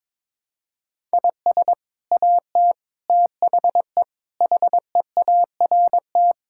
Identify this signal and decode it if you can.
Morse code, heard as a keyed tone, about 22 wpm, 700 Hz. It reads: IS AT THE HEART
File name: practice / 22wpm / Sets of 6 Words - 22wpm